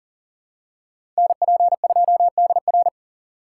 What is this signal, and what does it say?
Morse code, heard as a keyed tone, about 39 wpm, 700 Hz. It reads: NP2DR
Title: USA Call Signs - 4x Character Spacing - 40wpm